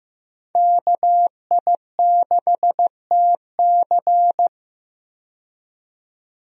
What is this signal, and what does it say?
Morse code, heard as a keyed tone, about 15 wpm, 700 Hz. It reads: KI6TC
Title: USA Call Signs - 5 Characters - 15wpm